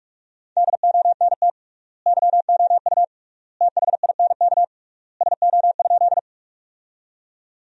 Morse code, heard as a keyed tone, about 46 wpm, 700 Hz. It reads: DONT YOU THINK SO?
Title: 3x Word-Spacing - Sentences from Top 300 Words - 45wpm